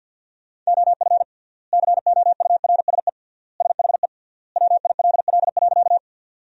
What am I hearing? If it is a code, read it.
Morse code, heard as a keyed tone, about 50 wpm, 700 Hz. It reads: OF COURSE SHE WILL.
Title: 3x Word-Spacing - Sentences from Top 500 Words - 50wpm